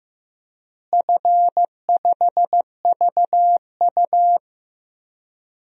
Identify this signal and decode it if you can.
Morse code, heard as a keyed tone, about 15 wpm, 700 Hz. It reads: F5VU
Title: DXCC-100 Call Signs - 4 Characters - Repeated 2x - 15wpm